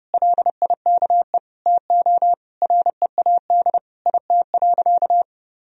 Morse code, heard as a keyed tone, about 31 wpm, 700 Hz. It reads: LIKE TO READ IT.
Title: Sentences from Top 500 Words and Abbreviations 30wpm